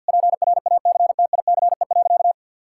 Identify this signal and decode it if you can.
Morse code, heard as a keyed tone, about 50 wpm, 700 Hz. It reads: PRACTICE.
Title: Sentences from Top 1000 Words - Review Part 5 - 50wpm